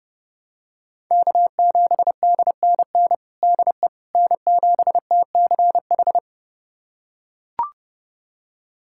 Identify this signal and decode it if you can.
Morse code, heard as a keyed tone, about 30 wpm, 700 Hz. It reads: K7DNN DE N7TCH
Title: USA Callsign DE Callsign - 5 Characters - 30wpm